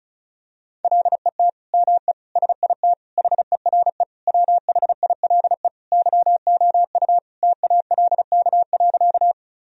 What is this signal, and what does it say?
Morse code, heard as a keyed tone, about 34 wpm, 700 Hz. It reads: LET ME SIT HERE WHILE YOU TALK.
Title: Sentences from Top 1000 Words - Review Part 1 - 35wpm